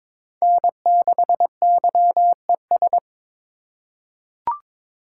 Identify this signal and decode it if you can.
Morse code, heard as a keyed tone, about 22 wpm, 700 Hz. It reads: N6YES
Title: USA Call Signs - 3x Character Spacing - 22wpm